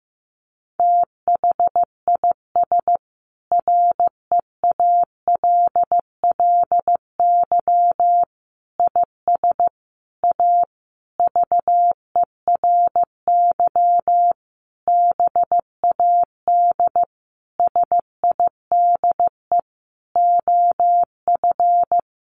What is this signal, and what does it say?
Morse code, heard as a keyed tone, about 15 wpm, 700 Hz. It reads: THIS REALLY IS A VERY BAD SIDE OF
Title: Sentences from Top 700 Words 15wpm